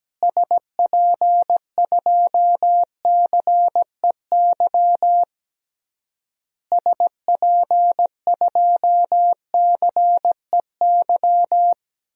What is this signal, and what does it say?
Morse code, heard as a keyed tone, about 17 wpm, 700 Hz. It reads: SP2CEY SP2CEY
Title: DXCC-100 Call Signs - 6 Characters - Repeated 3x - 17wpm